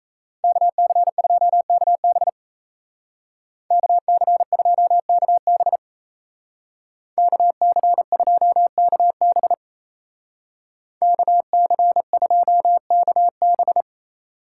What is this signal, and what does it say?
Morse code, heard as a keyed tone, about 34 wpm, 700 Hz. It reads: KC2KB KC2KB KC2KB KC2KB
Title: Speed Racing - USA Call Signs 28wpm